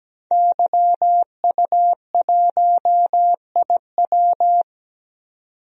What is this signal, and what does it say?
Morse code, heard as a keyed tone, about 17 wpm, 700 Hz. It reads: YU1IW
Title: DXCC-100 Call Signs - 5 Characters - 17wpm